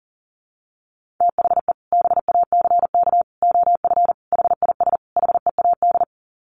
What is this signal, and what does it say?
Morse code, heard as a keyed tone, about 40 wpm, 700 Hz. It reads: THE BACK OF HIS HEAD
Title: Sets of 5 Words Encore 40wpm